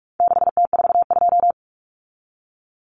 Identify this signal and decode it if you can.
Morse code, heard as a keyed tone, about 44 wpm, 700 Hz. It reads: 6T42
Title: Common Cut Numbers - 4-Digit 45wpm